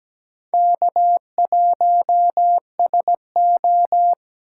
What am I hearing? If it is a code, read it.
Morse code, heard as a keyed tone, about 20 wpm, 700 Hz. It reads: K1SO
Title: Speed Racing - USA 1x2 Call Signs 17wpm